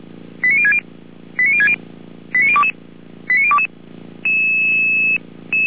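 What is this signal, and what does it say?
Five-tone selective call (ZVEI), heard as quick runs of five tones, tones of about 70 ms. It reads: 89060 89060 89022 89022
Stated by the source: ZVEI five-tone selective call received on an SDR, from a fire brigade in Germany.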